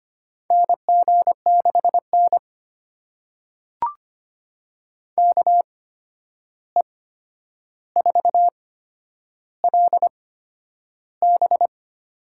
Morse code, heard as a keyed tone, about 25 wpm, spaced as for 10 wpm, 700 Hz. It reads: NG6N KE4LB
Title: USA Call Signs - 8x Character Spacing - 25wpm